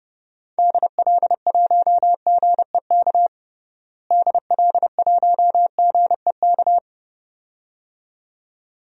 Morse code, heard as a keyed tone, about 30 wpm, 700 Hz. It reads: DL1GEK DL1GEK
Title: DXCC-100 Call Signs - 6 Characters - Repeated 2x - 30wpm